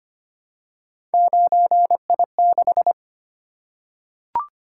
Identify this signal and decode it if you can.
Morse code, heard as a keyed tone, about 25 wpm, 700 Hz. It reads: 9I6
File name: Three Letter-Number Combinations - 25wpm